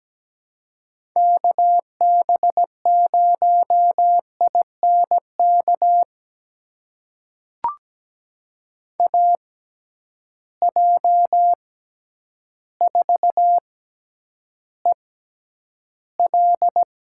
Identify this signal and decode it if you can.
Morse code, heard as a keyed tone, about 17 wpm, spaced as for 9 wpm, 700 Hz. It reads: KB0INK AJ4EL
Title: USA Call Signs - 6x Character Spacing - 17wpm